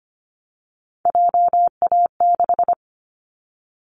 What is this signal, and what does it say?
Morse code, heard as a keyed tone, about 25 wpm, 700 Hz. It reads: JA6
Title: Callsign Prefixes for Top 50 DXCC - 25wpm